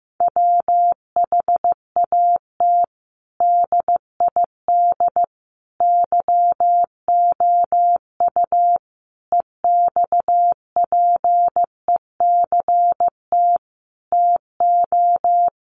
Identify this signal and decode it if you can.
Morse code, heard as a keyed tone, about 15 wpm, 700 Hz. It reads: WHAT DID YOU EXPECT TO
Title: Sentences from Top 1100 Words 15wpm